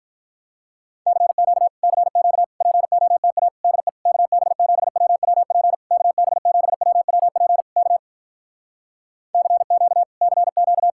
Morse code, heard as a keyed tone, about 46 wpm, 700 Hz. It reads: CQ CQ POTA DE KB7WWP KB7WWP K CQ CQ